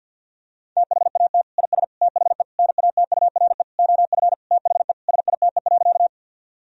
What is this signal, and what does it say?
Morse code, heard as a keyed tone, about 50 wpm, 700 Hz. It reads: THAT IS THE NATURE OF THE SITE.
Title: Sentences from Top 1000 Words - Review Part 4 - 50wpm